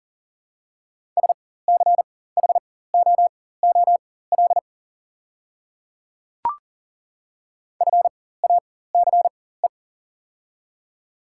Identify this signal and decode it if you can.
Morse code, heard as a keyed tone, about 40 wpm, spaced as for 19 wpm, 690 Hz. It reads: SCHOOL FACE
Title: 4x Character Spacing - Top 500 Words - 40wpm